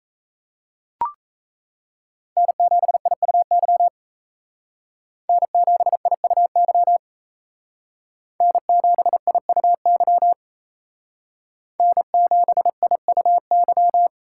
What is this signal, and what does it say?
Morse code, heard as a keyed tone, about 35 wpm, 700 Hz. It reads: N7IUY N7IUY N7IUY N7IUY